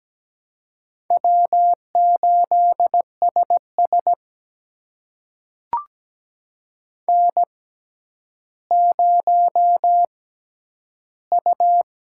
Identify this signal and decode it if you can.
Morse code, heard as a keyed tone, about 17 wpm, spaced as for 10 wpm, 700 Hz. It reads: W8SS N0U
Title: USA Call Signs - 4 Characters - 6x Character Spacing - 17wpm